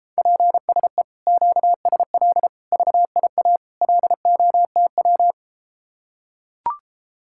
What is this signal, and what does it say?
Morse code, heard as a keyed tone, about 33 wpm, 700 Hz. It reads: PSE QSL VIA LOTW